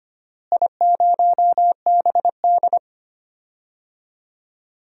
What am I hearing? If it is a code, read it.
Morse code, heard as a keyed tone, about 25 wpm, 700 Hz. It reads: I0BD
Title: DXCC-100 Call Signs - 4 Characters - Repeated 2x - 25wpm